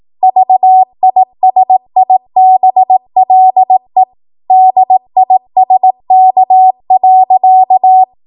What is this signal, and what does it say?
Morse code, heard as a keyed tone, about 18 wpm, 750 Hz. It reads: VISIBLE DISK.
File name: Morse / 18WPM